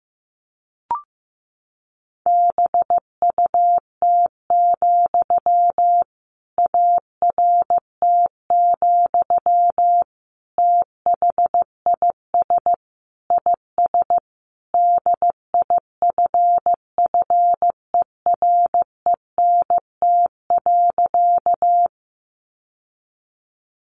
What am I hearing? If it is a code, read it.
Morse code, heard as a keyed tone, about 15 wpm, 700 Hz. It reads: BUT, ART, THIS IS DIFFERENT.
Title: Sentences from Top 600 Words 15wpm